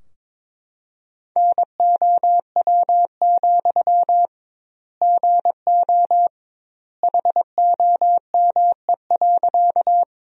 Morse code, heard as a keyed tone, about 22 wpm, 700 Hz. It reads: NOW, GO HOME.
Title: 2x Word-Spacing - Sentences from Top 200 Words - 22wpm